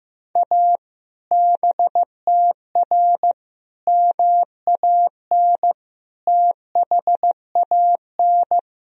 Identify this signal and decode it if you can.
Morse code, heard as a keyed tone, about 15 wpm, 700 Hz. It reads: A BTR MAN THAN
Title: Sentences from Top 400 Words and Abbreviations 15wpm